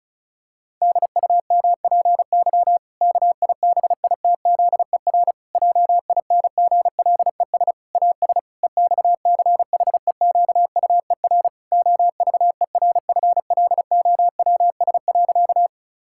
Morse code, heard as a keyed tone, about 36 wpm, 700 Hz. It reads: DUMPY KIBITZER JINGLES AS EXCHEQUER OVERFLOWS.
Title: Pangram Challenge v2 - 35wpm